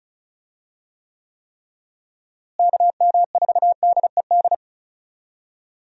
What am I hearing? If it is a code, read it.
Morse code, heard as a keyed tone, about 34 wpm, 700 Hz. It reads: KM4DED
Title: USA Call Signs - 6 Characters - Repeated 3x - 35wpm